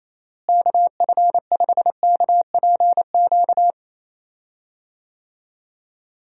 Morse code, heard as a keyed tone, about 28 wpm, 700 Hz. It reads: KF5KPQ
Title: USA Call Signs - 6 Characters - Repeated 2x - 28wpm